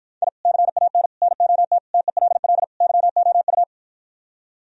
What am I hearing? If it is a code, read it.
Morse code, heard as a keyed tone, about 52 wpm, 700 Hz. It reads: I CAN NOT TELL YOU